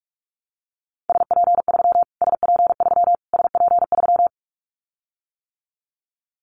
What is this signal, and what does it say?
Morse code, heard as a keyed tone, about 44 wpm, 700 Hz. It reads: SP3 SP3 SP3
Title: Callsign Prefixes for Top 25 DXCC - Repeated 3x - 45wpm